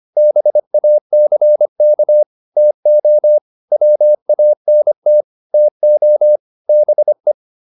Morse code, heard as a keyed tone, about 25 wpm, 600 Hz. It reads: BACK TO WANT TO BE